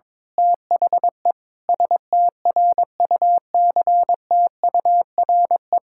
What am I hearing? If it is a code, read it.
Morse code, heard as a keyed tone, about 22 wpm, 700 Hz. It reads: THE STRUCTURE
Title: Sentences from Top 1000 Words - Review Part 6 - 22wpm